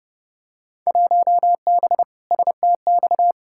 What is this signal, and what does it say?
Morse code, heard as a keyed tone, about 30 wpm, 700 Hz. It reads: 1B STX
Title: Field Day Exchanges - 30wpm